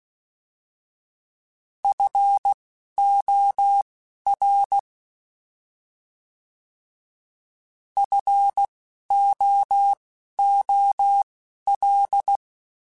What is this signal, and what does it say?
Morse code, heard as a keyed tone, about 16 wpm, spaced as for 11 wpm, 800 Hz. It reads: FOR FOOL
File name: Alphabetical / Alphabetical mp3s / Fs @10